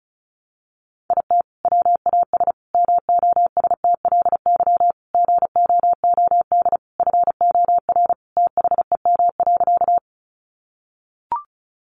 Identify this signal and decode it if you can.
Morse code, heard as a keyed tone, about 34 wpm, 700 Hz. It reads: IT WAS MOSTLY GOOD FOR THEM.
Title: Sentences from Top 1300 Words 35wpm